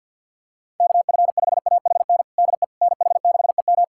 Morse code, heard as a keyed tone, about 53 wpm, 700 Hz. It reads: KF5AHN DE NH6EM
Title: USA Callsign DE Callsign 50wpm